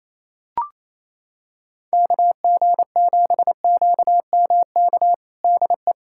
Morse code, heard as a keyed tone, about 28 wpm, 700 Hz. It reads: KG7QMK DE
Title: USA Callsign DE Callsign - 6 Characters - 28wpm